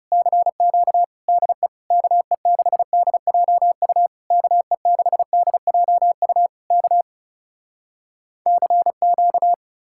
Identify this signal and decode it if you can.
Morse code, heard as a keyed tone, about 37 wpm, 700 Hz. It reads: CQ DE KE6DJU KE6DJU K CQ
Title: Speed-Racing Common QSO Phrases - 30wpm